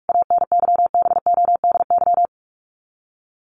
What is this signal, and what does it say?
Morse code, heard as a keyed tone, about 44 wpm, 700 Hz. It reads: ANYBODY
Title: Sets of 5 Words 45wpm